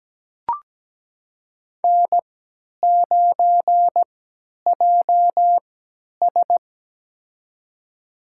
Morse code, heard as a keyed tone, about 17 wpm, spaced as for 10 wpm, 700 Hz. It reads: N9JS
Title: USA Call Signs - 4 Characters - 3x Character Spacing - 17wpm